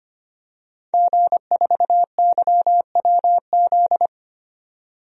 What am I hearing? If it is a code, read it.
Morse code, heard as a keyed tone, about 25 wpm, 700 Hz. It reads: G4YWZ
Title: DXCC-100 Call Signs - 5 Characters - 25wpm